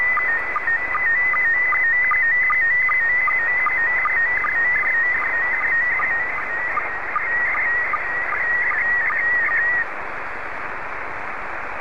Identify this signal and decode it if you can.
SSTV (slow-scan television), PD 50.
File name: sstv / PD50